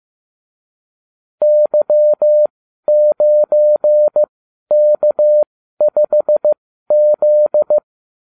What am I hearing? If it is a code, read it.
Morse code, heard as a keyed tone, about 15 wpm, spaced as for 12 wpm, 600 Hz. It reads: Y9K5Z